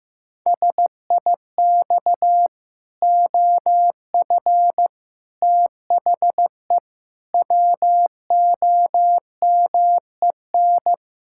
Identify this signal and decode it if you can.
Morse code, heard as a keyed tone, about 15 wpm, 700 Hz. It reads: SIX OF THE WOMEN
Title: Pangram Challenge v2 - 15wpm